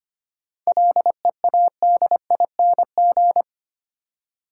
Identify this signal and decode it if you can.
Morse code, heard as a keyed tone, about 25 wpm, 700 Hz. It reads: LEADING
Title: Seven Letter Words - 25wpm